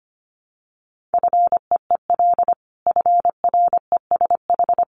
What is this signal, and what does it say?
Morse code, heard as a keyed tone, about 25 wpm, 700 Hz. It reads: FEEL FRESH
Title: Sentences from Top 1200 Words 25wpm